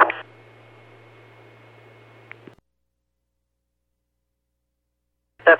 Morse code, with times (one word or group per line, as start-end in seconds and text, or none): none